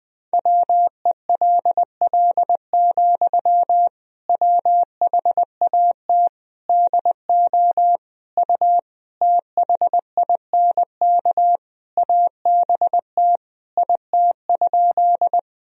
Morse code, heard as a keyed tone, about 20 wpm, 700 Hz. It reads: WELL, WHAT DO U THINK ABT IT?
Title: Sentences from Top 200 Words and Abbreviations 20wpm